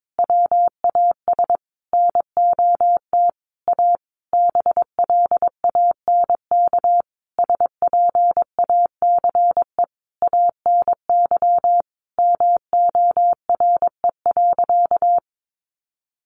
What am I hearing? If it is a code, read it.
Morse code, heard as a keyed tone, about 22 wpm, 700 Hz. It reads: WAS NOT A BLANK SPACE ANY MORE.
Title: Sentences from Top 1500 Words 22wpm